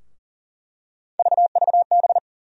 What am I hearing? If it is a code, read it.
Morse code, heard as a keyed tone, about 40 wpm, 700 Hz. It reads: VVB
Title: USA Call Sign Postfixes - 3 Letters 40wpm